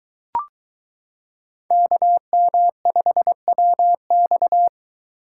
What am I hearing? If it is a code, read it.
Morse code, heard as a keyed tone, about 22 wpm, 700 Hz. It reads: KM5WX